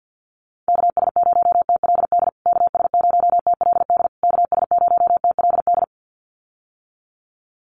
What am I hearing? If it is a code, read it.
Morse code, heard as a keyed tone, about 52 wpm, 700 Hz. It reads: KS0TLD KS0TLD KS0TLD
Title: USA Call Signs Repeated 3x 50wpm